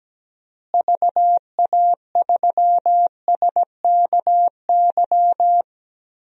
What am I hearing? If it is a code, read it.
Morse code, heard as a keyed tone, about 17 wpm, 700 Hz. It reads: VA3SKY
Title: DXCC-100 Call Signs - 6 Characters - Repeated 3x - 17wpm